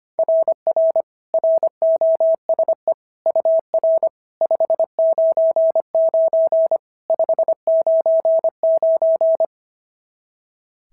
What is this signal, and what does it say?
Morse code, heard as a keyed tone, about 25 wpm, 690 Hz. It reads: RR ROSE UR 599 599
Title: SOTA QSO Practice 25wpm